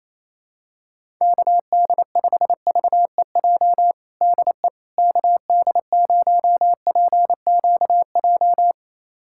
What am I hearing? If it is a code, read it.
Morse code, heard as a keyed tone, about 28 wpm, 700 Hz. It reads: KD5VEJ DE KD0PQJ
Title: USA Callsign DE Callsign - 6 Characters - 28wpm